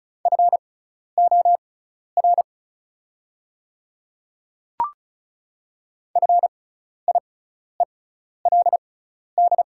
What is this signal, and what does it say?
Morse code, heard as a keyed tone, about 36 wpm, spaced as for 12 wpm, 700 Hz. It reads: FOR FIELD
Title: 6x Character Spacing - Top 500 Words - 35wpm